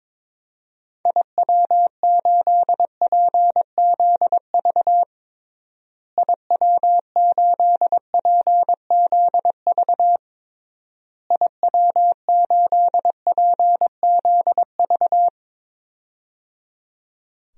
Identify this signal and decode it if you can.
Morse code, heard as a keyed tone, about 22 wpm, 700 Hz. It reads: IW8PZV IW8PZV IW8PZV
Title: DXCC-100 Call Signs - 6 Characters - Repeated 3x - 22wpm